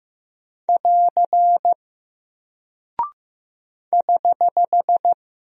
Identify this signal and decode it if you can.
Morse code, heard as a keyed tone, about 15 wpm, 700 Hz. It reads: <AR> <HH>
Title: Single Character (Letter:Number:Punctuation:Prosign) Alt 15wpm